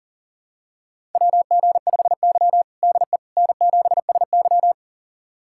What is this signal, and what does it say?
Morse code, heard as a keyed tone, about 40 wpm, 700 Hz. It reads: WG5Y DE N7SY
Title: USA Callsign DE Callsign - 4 Characters - 40wpm